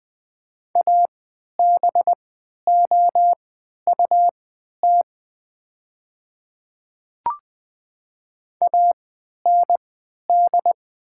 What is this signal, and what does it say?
Morse code, heard as a keyed tone, about 20 wpm, spaced as for 11 wpm, 700 Hz. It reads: ABOUT AND
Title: 3x Character Spacing - Top 200 Words - 20wpm